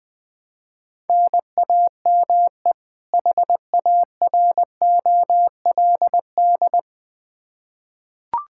Morse code, heard as a keyed tone, about 20 wpm, 700 Hz. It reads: NAME HAROLD
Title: Practice QSO - Exchange 1 - Beginning - 20wpm